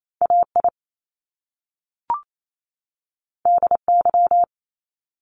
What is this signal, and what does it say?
Morse code, heard as a keyed tone, about 28 wpm, 700 Hz. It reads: AI DY
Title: Top Two Letter Combinations in Words 28wpm